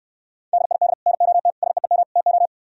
Morse code, heard as a keyed tone, about 67 wpm, 700 Hz. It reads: DID NOT SEEM TO